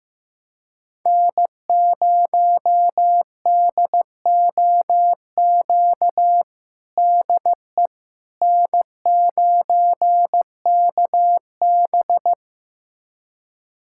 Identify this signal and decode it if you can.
Morse code, heard as a keyed tone, about 15 wpm, 700 Hz. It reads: N0DOQ DE N9KB
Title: USA Callsign DE Callsign 15wpm